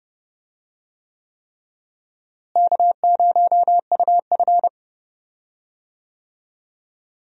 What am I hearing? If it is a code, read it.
Morse code, heard as a keyed tone, about 30 wpm, 700 Hz. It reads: K0UF
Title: USA Call Signs - 4 Characters - Repeated 3x - 30wpm